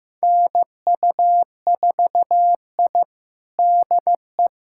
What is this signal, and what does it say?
Morse code, heard as a keyed tone, about 15 wpm, 700 Hz. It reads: NU4I DE
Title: USA Callsign DE Callsign - 4 Characters - 15wpm